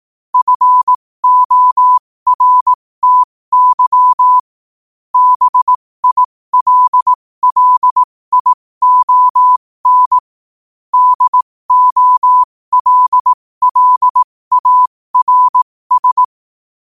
Morse code, heard as a keyed tone, about 18 wpm, spaced as for 15 wpm, 1000 Hz. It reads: FORTY BILLION DOLLARS
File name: qotdfiles / 15wpm